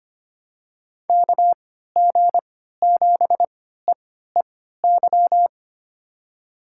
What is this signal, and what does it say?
Morse code, heard as a keyed tone, about 25 wpm, spaced as for 14 wpm, 700 Hz. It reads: KG7EEY
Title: USA Call Signs - 6 Characters - 3x Character Spacing - 25wpm